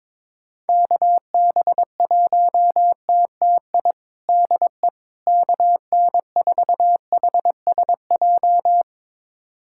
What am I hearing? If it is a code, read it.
Morse code, heard as a keyed tone, about 22 wpm, 700 Hz. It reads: KB1TTI DE KN4HSJ
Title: USA Callsign DE Callsign 22wpm